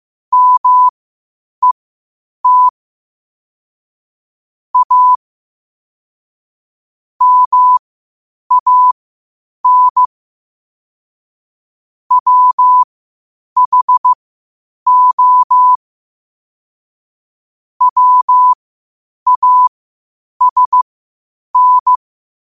Morse code, heard as a keyed tone, about 15 wpm, spaced as for 8 wpm, 1000 Hz. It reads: MET A MAN WHO WASN
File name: qotdfiles / 8wpm